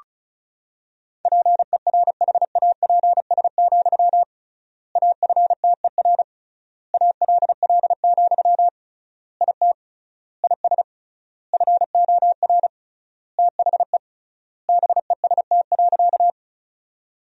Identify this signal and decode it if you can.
Morse code, heard as a keyed tone, about 34 wpm, 700 Hz. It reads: PERHAPS, AFTER ALL, IT IS FOR THE BEST.
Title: 3x Word-Spacing - Sentences from Top 500 Words - 35wpm